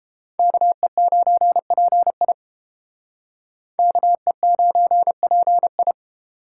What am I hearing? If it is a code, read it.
Morse code, heard as a keyed tone, about 30 wpm, 700 Hz. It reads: KE9PI KE9PI